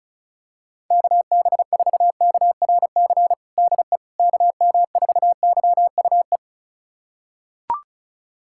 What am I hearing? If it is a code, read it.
Morse code, heard as a keyed tone, about 36 wpm, 700 Hz. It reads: KB4KRC DE KM4YUE
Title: USA Callsign DE Callsign - 6 Characters - 35wpm